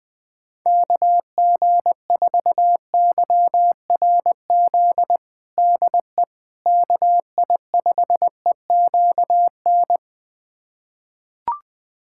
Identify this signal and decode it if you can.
Morse code, heard as a keyed tone, about 21 wpm, 700 Hz. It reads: KG4YRZ DE KI5EQN
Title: USA Callsign DE Callsign - 6 Characters - 20wpm